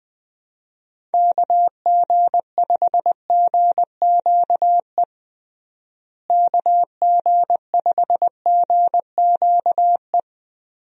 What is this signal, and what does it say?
Morse code, heard as a keyed tone, about 20 wpm, 700 Hz. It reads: KG5GQE KG5GQE
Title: USA Call Signs - 6 Characters - Repeated 3x - 20wpm